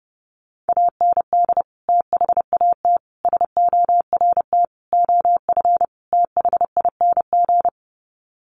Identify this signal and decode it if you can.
Morse code, heard as a keyed tone, about 30 wpm, 700 Hz. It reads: AND THAT SORT OF THING
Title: Sets of 5 Words Encore2 30wpm